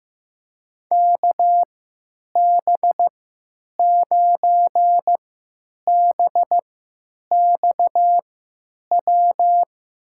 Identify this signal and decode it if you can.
Morse code, heard as a keyed tone, about 16 wpm, spaced as for 9 wpm, 700 Hz. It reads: KB9BXW
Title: USA Call Signs - 6 Characters - 3x Character Spacing - 15wpm